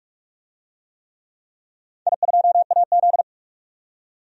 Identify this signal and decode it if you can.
Morse code, heard as a keyed tone, about 44 wpm, 690 Hz. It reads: I2AZ
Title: DXCC-100 Call Signs - 4 Characters - 45wpm